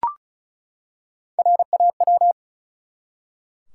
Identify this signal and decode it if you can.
Morse code, heard as a keyed tone, about 34 wpm, 700 Hz. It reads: RAW
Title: Three Letter Words - 35wpm